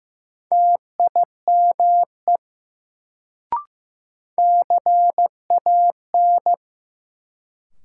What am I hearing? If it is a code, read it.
Morse code, heard as a keyed tone, about 15 wpm, 700 Hz. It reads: TIME CAN
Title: Sentences from Top 100 Words - New Word Drill - 15wpm